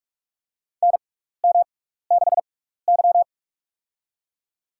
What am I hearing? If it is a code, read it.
Morse code, heard as a keyed tone, about 46 wpm, spaced as for 15 wpm, 700 Hz. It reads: NM6Y
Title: USA Call Signs - 6x Character Spacing - 45wpm